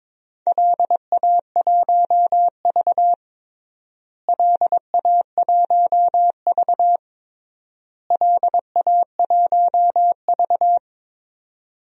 Morse code, heard as a keyed tone, about 22 wpm, 700 Hz. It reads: LA1V LA1V LA1V
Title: DXCC-100 Call Signs - 4 Characters - Repeated 3x - 22wpm